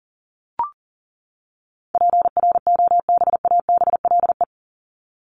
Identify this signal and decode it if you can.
Morse code, heard as a keyed tone, about 37 wpm, 700 Hz. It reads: PROBABLE
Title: LY Suffix - Top 100 Words with Base Words 40wpm